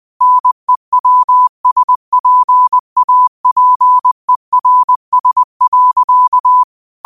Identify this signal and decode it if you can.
Morse code, heard as a keyed tone, about 20 wpm, 1000 Hz. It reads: NEWSPAPERS.
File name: qotdfiles / 20wpm